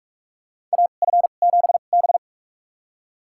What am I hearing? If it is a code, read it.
Morse code, heard as a keyed tone, about 45 wpm, spaced as for 32 wpm, 700 Hz. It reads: AF7B
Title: USA Call Signs - 4 Characters - 2x Character Spacing - 45wpm